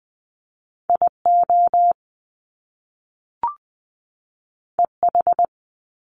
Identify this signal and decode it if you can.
Morse code, heard as a keyed tone, about 20 wpm, 700 Hz. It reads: IO EH
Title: Top Two Letter Combinations in Words 20wpm